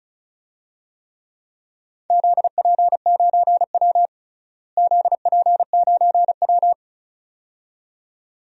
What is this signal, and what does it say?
Morse code, heard as a keyed tone, about 36 wpm, 700 Hz. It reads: ZP9W ZP9W
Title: DXCC-100 Call Signs - 4 Characters - Repeated 2x - 35wpm